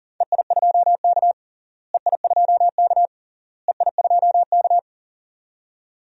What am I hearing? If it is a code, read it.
Morse code, heard as a keyed tone, about 40 wpm, 700 Hz. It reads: EI2K EI2K EI2K
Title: DXCC-100 Call Signs - 4 Characters - Repeated 3x - 40wpm